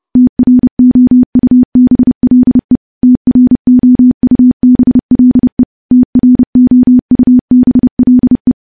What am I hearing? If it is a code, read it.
Morse code, heard as a keyed tone, about 32 wpm, 250 Hz. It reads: TROUBLE TROUBLE TROUBLE
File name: morse / morse-training-30WPM